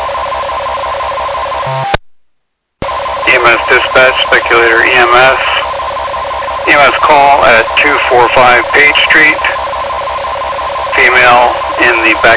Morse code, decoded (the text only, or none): none